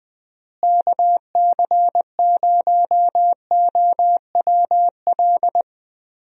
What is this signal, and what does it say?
Morse code, heard as a keyed tone, about 20 wpm, 700 Hz. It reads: KC0OWL
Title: USA Call Signs - 6 Characters - Repeated 3x - 20wpm